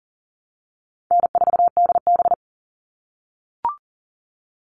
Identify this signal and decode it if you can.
Morse code, heard as a keyed tone, about 40 wpm, 700 Hz. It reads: N4DB
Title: All Cut Numbers - 4-Digit 40wpm